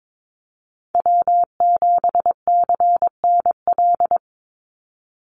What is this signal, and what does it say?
Morse code, heard as a keyed tone, about 22 wpm, 700 Hz. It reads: W7CNL
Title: NA SOTA Chaser Callsigns 22wpm